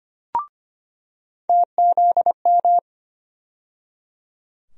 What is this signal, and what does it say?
Morse code, heard as a keyed tone, about 25 wpm, 700 Hz. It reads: TZM
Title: USA Call Sign Postfixes - 3 Letters 25wpm